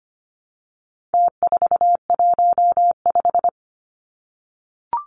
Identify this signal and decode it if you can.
Morse code, heard as a keyed tone, about 25 wpm, 700 Hz. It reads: T415
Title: Numbers - Contest Style - 4-Digit 25wpm